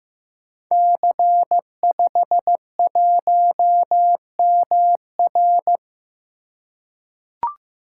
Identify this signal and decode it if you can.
Morse code, heard as a keyed tone, about 15 wpm, 700 Hz. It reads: C51MR
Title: Code Groups - 5 Letter:Numbers - 15wpm